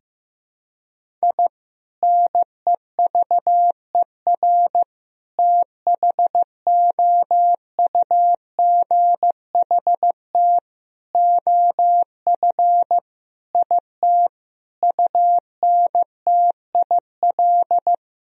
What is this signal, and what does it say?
Morse code, heard as a keyed tone, about 15 wpm, 700 Hz. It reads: I NEVER THOUGHT OF IT UNTIL
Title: Sentences from Top 1000 Words - Review Part 1 - 15wpm